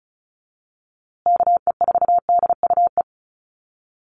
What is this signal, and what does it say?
Morse code, heard as a keyed tone, about 34 wpm, 700 Hz. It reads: KE4DUE
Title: USA Call Signs Repeated 2x 35wpm